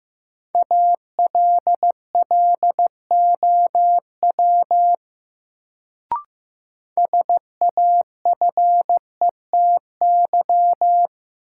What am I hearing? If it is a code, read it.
Morse code, heard as a keyed tone, about 15 wpm, 700 Hz. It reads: ALLOW SAFETY
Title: Sentences from Top 700 Words - New Word Drill - 15wpm